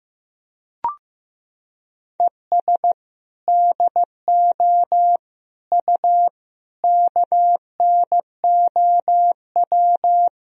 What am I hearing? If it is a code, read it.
Morse code, heard as a keyed tone, about 15 wpm, 700 Hz. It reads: ES DO U KNOW
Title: Sentences from Top 200 Words and Abbreviations 15wpm